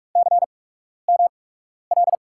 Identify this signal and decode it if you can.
Morse code, heard as a keyed tone, about 44 wpm, spaced as for 12 wpm, 700 Hz. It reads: CML
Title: USA Call Signs - 8x Character Spacing - 45wpm